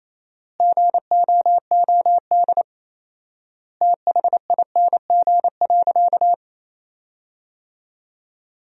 Morse code, heard as a keyed tone, about 28 wpm, 700 Hz. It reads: GOOD THING.
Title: 4x Word-Spacing - Sentences from Top 300 Words - 28wpm